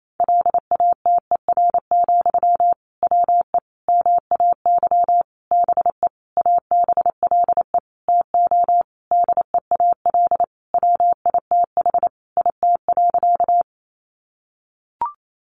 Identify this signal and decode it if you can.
Morse code, heard as a keyed tone, about 28 wpm, 700 Hz. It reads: LATER, WE MAY BE ABLE TO DEAL WITH IT.
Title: Sentences from Top 800 Words 28wpm